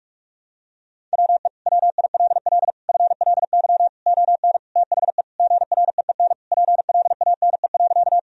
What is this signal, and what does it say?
Morse code, heard as a keyed tone, about 41 wpm, 700 Hz. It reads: WE WILL FLY ON THE GREEN PLANE.
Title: Sentences from Top 1000 Words - Review Part 3 - 45wpm